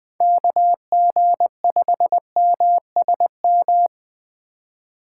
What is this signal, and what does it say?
Morse code, heard as a keyed tone, about 20 wpm, 700 Hz. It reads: KG5MSM
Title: USA Call Signs - 6 Characters - 6x Character Spacing - 20wpm